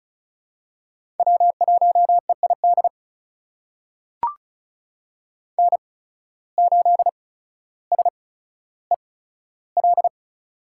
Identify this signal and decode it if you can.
Morse code, heard as a keyed tone, about 36 wpm, spaced as for 15 wpm, 700 Hz. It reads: W1EID N8SEL
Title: USA Call Signs - 8x Character Spacing - 35wpm